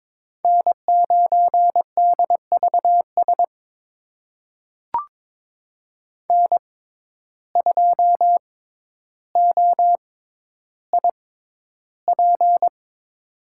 Morse code, heard as a keyed tone, about 22 wpm, spaced as for 11 wpm, 700 Hz. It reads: N9DVS N2OIP
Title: USA Call Signs - 5 Characters - 6x Character Spacing - 22wpm